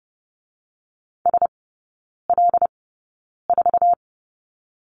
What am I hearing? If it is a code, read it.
Morse code, heard as a keyed tone, about 30 wpm, 700 Hz. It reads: S L 4
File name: Lesson 19e - Letter w - Three Characters 30wpm